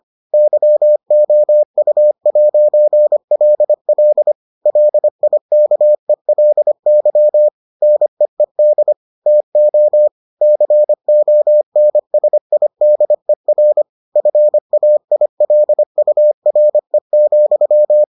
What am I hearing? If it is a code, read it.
Morse code, heard as a keyed tone, about 25 wpm, 600 Hz. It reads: YOU'LL LIKELY NEED TO CONSIDER FAILURE,